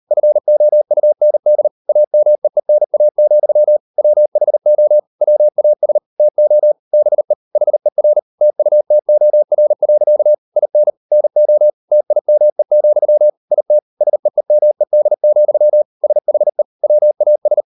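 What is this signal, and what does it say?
Morse code, heard as a keyed tone, about 38 wpm, 600 Hz. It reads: FOUND AMEENA, WHO WAS TO BE HER TUTOR. IN NO TIME, IT SEEMED, SHE WAS
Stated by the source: Morse code practice at 38 WPM, sending Ken Follett's novel Never in English, from CW med Gnister.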